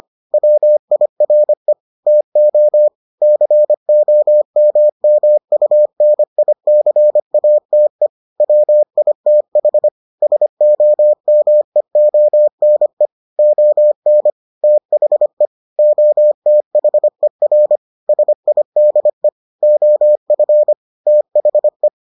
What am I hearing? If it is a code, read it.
Morse code, heard as a keyed tone, about 25 wpm, 600 Hz. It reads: WIRE TO COMMUNICATE WITH SOMEONE ON THE OTHER SIDE OF THE